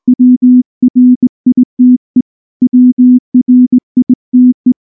Morse code, heard as a keyed tone, about 19 wpm, 250 Hz. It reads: WRITE WRITE
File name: morse / morse-training-20WPM-1